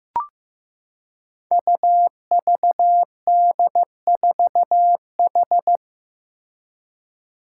Morse code, heard as a keyed tone, about 15 wpm, 700 Hz. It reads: UVD4H
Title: Five Challenging Letter-Number Combinations 15wpm